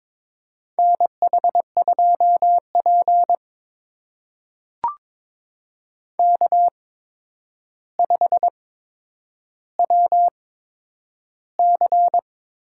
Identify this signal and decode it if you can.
Morse code, heard as a keyed tone, about 22 wpm, spaced as for 9 wpm, 700 Hz. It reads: NH2P K5WC
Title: USA Call Signs - 8x Character Spacing - 22wpm